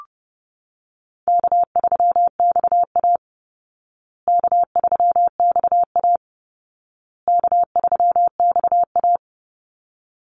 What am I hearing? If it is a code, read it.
Morse code, heard as a keyed tone, about 30 wpm, 700 Hz. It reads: K3XA K3XA K3XA
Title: Easy Call Signs Repeated 3x - 30wpm